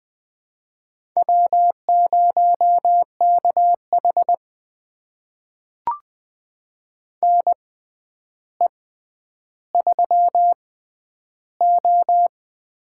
Morse code, heard as a keyed tone, about 20 wpm, spaced as for 10 wpm, 700 Hz. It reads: W0KH NE3O
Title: USA Call Signs - 6x Character Spacing - 20wpm